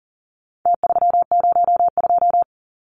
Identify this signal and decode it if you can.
Morse code, heard as a keyed tone, about 40 wpm, 700 Hz. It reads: T302
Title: Numbers - Contest Style - 4-Digit 40wpm